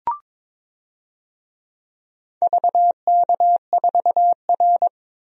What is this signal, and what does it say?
Morse code, heard as a keyed tone, about 22 wpm, 700 Hz. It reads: VK4R
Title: DXCC-100 Call Signs - 4 Characters - 22wpm